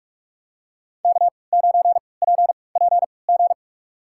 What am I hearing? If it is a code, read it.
Morse code, heard as a keyed tone, about 46 wpm, spaced as for 25 wpm, 700 Hz. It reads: K9PPG
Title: USA Call Signs - 3x Character Spacing - 45wpm